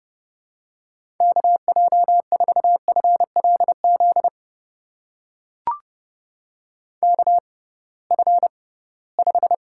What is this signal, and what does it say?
Morse code, heard as a keyed tone, about 30 wpm, spaced as for 19 wpm, 700 Hz. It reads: KJ4FLZ KF5
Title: USA Call Signs - 6x Character Spacing - 30wpm